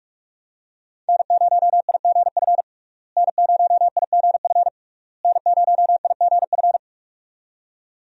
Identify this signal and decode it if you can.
Morse code, heard as a keyed tone, about 46 wpm, 700 Hz. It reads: N0IGF N0IGF N0IGF
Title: USA Call Signs - 5 Characters - Repeated 3x - 45wpm